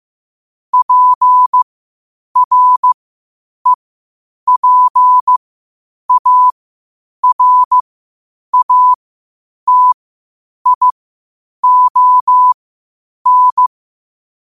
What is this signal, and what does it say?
Morse code, heard as a keyed tone, about 15 wpm, spaced as for 8 wpm, 1000 Hz. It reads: PREPARATION
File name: qotdfiles / 8wpm